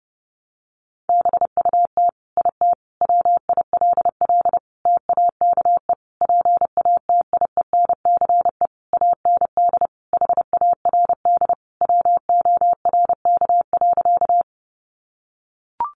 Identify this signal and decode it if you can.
Morse code, heard as a keyed tone, about 30 wpm, 700 Hz. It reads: BUT IT WILL TAKE PATIENCE AND HARD WORK.
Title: Sentences from Top 1300 Words 30wpm